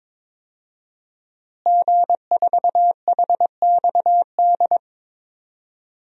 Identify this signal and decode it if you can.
Morse code, heard as a keyed tone, about 22 wpm, 700 Hz. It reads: G4HXD
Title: DXCC-100 Call Signs - 5 Characters - Repeated 2x - 22wpm